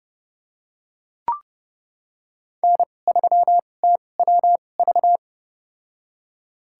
Morse code, heard as a keyed tone, about 30 wpm, spaced as for 22 wpm, 700 Hz. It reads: N3TWV
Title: USA Call Signs - 5 Characters - 2x Character Spacing - 30wpm